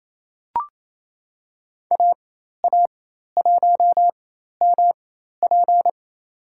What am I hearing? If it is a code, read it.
Morse code, heard as a keyed tone, about 28 wpm, spaced as for 13 wpm, 700 Hz. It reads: AA1MP
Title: USA Call Signs - 5 Characters - 4x Character Spacing - 28wpm